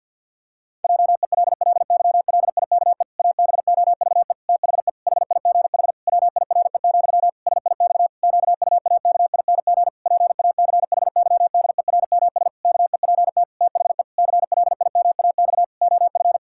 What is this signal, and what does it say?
Morse code, heard as a keyed tone, about 50 wpm, 700 Hz. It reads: JELLYLIKE ABOVE THE HIGH WIRE, SIX QUAKING PACHYDERMS KEPT THE CLIMAX OF
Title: Pangram Challenge v2 - 50wpm